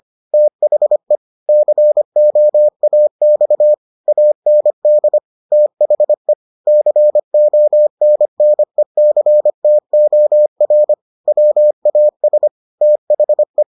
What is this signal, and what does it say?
Morse code, heard as a keyed tone, about 25 wpm, 600 Hz. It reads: THE COAX AND THE CONNECTOR WAS THE